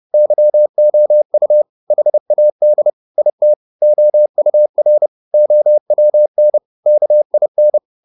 Morse code, heard as a keyed tone, about 30 wpm, 600 Hz. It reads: YOU HAD IT OUR OWN KIN